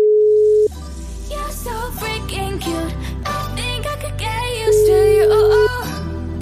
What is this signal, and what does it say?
Ringback tone (Europe, 425 Hz).